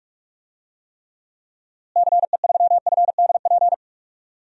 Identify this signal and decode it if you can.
Morse code, heard as a keyed tone, about 46 wpm, 700 Hz. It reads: CE3FDP
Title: DXCC-100 Call Signs - 6 Characters - Repeated 2x - 45wpm